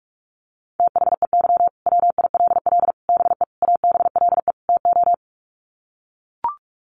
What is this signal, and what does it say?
Morse code, heard as a keyed tone, about 44 wpm, 700 Hz. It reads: THEY WILL BE ABLE TO
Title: Sets of 5 Words Encore 45wpm